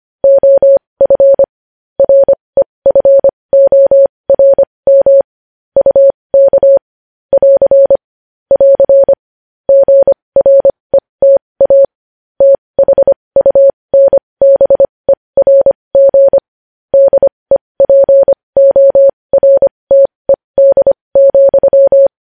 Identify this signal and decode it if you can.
Morse code, heard as a keyed tone, about 25 wpm, spaced as for 20 wpm, 550 Hz. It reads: OF REFORM UK <AR> <AR> GRETA THUNBERG DEPORTED,
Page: News Headlines in Morse Code at 20 WPM